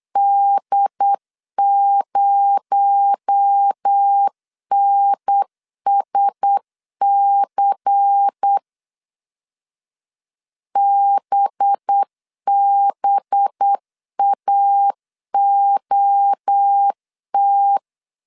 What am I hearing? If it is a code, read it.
Morse code, heard as a keyed tone, about 8 wpm, 800 Hz. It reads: D0NSC BBAOT
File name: CW / DET / 1 6 1500C S B 8wpm 5142